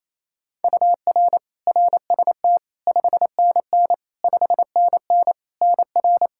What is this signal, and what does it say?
Morse code, heard as a keyed tone, about 28 wpm, 700 Hz. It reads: UR RST 5NN 5NN NR
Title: K1USN SST QSO Practice 28wpm